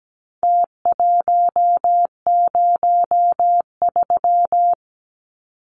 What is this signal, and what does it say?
Morse code, heard as a keyed tone, about 17 wpm, 700 Hz. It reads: T103
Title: Numbers - Contest Style - 4-Digit 17wpm